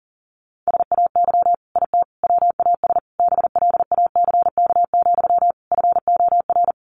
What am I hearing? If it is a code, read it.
Morse code, heard as a keyed tone, about 40 wpm, 700 Hz. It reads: SAY IT WAS BLACK, FOR
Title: Sentences from Top 800 Words 40wpm